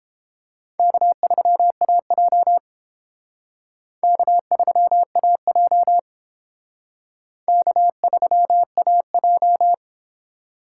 Morse code, heard as a keyed tone, about 30 wpm, 700 Hz. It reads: K3AJ K3AJ K3AJ